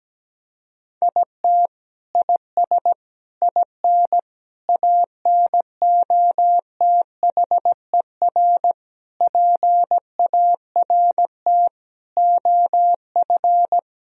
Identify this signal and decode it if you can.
Morse code, heard as a keyed tone, about 17 wpm, 700 Hz. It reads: IT IS IN ANOTHER PART OF
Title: Sentences from Top 1000 Words - Review Part 2 - 17wpm